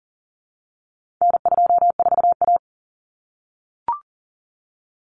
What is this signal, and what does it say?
Morse code, heard as a keyed tone, about 40 wpm, 700 Hz. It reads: N24A
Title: Common Cut Numbers - 4-Digit 40wpm